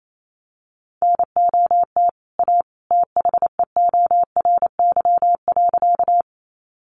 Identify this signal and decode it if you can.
Morse code, heard as a keyed tone, about 28 wpm, 700 Hz. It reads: NOT A THEORY.
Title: Sentences from Top 600 Words 28wpm